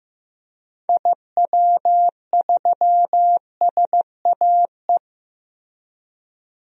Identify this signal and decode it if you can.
Morse code, heard as a keyed tone, about 15 wpm, 700 Hz. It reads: IW3SAE
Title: DXCC-100 Call Signs - 6 Characters - Repeated 3x - 15wpm